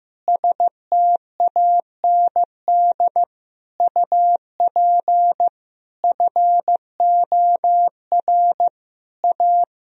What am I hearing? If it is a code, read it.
Morse code, heard as a keyed tone, about 15 wpm, 700 Hz. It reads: STAND UP FOR A
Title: Sentences from Top 1000 Words - Review Part 1 - 15wpm